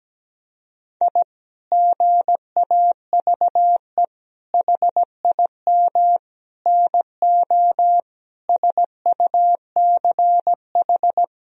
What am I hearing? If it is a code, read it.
Morse code, heard as a keyed tone, about 17 wpm, 700 Hz. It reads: I GAVE HIM NO SUCH
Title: Sentences from Top 1000 Words - Review Part 2 - 17wpm